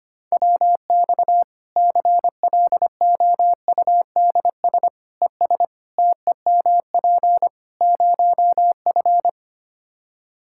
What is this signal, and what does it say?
Morse code, heard as a keyed tone, about 25 wpm, 700 Hz. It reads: WX CLOUDS ES TEMP 0F
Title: Practice QSO - Exchange 2 - Info - 25wpm